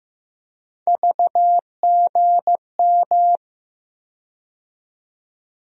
Morse code, heard as a keyed tone, about 15 wpm, 700 Hz. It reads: VGM
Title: USA Call Sign Postfixes - 3 Letters 15wpm